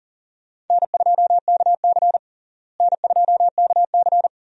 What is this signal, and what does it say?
Morse code, heard as a keyed tone, about 40 wpm, 700 Hz. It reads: N2KC N2KC
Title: USA Call Signs - 4 Characters - Repeated 3x - 40wpm